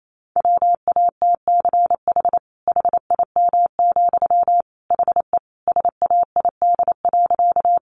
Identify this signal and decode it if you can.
Morse code, heard as a keyed tone, about 28 wpm, 700 Hz. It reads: WATCH HIM, HE SAID.
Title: Sentences from Top 300 Words 28wpm